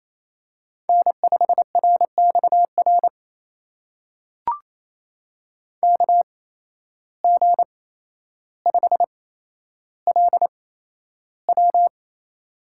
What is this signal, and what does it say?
Morse code, heard as a keyed tone, about 28 wpm, spaced as for 12 wpm, 700 Hz. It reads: N5RXR KG5LW
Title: USA Call Signs - 5 Characters - 8x Character Spacing - 28wpm